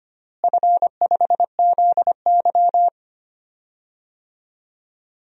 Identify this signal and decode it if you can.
Morse code, heard as a keyed tone, about 25 wpm, 700 Hz. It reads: F5ZY
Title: DXCC-100 Call Signs - 4 Characters - Repeated 2x - 25wpm